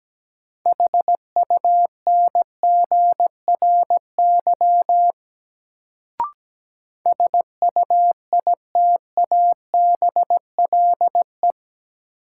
Morse code, heard as a keyed tone, about 17 wpm, 700 Hz. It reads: HUNGRY SUITABLE
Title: Sentences from Top 1400 Words - New Word Drill Alt - 17wpm